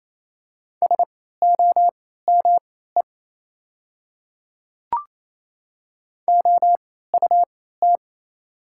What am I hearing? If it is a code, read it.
Morse code, heard as a keyed tone, about 27 wpm, spaced as for 16 wpm, 690 Hz. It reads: SOME OUT
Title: 3x Character Spacing - Top 100 Words - 28wpm